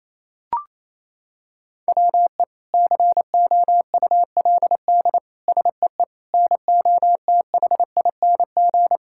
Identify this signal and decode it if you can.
Morse code, heard as a keyed tone, about 28 wpm, 700 Hz. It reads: WE COULD SEE NOTHING
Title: Sentences from Top 1000 Words - Review Part 5 - 28wpm